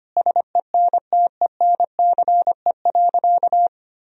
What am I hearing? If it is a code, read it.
Morse code, heard as a keyed tone, about 25 wpm, 700 Hz. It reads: SENTENCE.
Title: Sentences from Top 1000 Words - Review Part 1 - 25wpm